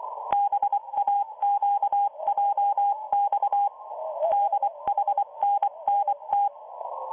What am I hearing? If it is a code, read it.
Morse code, heard as a keyed tone, about 24 wpm, 810 Hz, with radio noise. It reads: BAQJX DHNNT